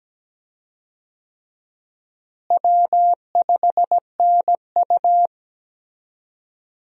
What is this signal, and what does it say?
Morse code, heard as a keyed tone, about 17 wpm, 700 Hz. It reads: W5NU
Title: USA Call Signs - 4 Characters - Repeated 2x - 17wpm